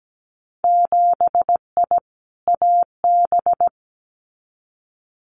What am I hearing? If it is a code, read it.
Morse code, heard as a keyed tone, about 17 wpm, 700 Hz. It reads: 7I AB
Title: Winter Field Day Exchanges Encore 17wpm